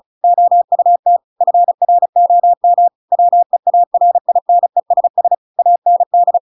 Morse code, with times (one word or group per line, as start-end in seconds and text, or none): 0.2-1.2 OUT
1.4-2.9 FROM
3.1-5.3 WEARINESS
5.6-6.4 AND